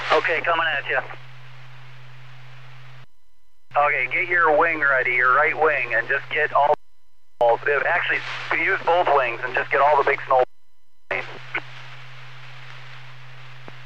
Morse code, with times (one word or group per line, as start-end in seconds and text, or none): none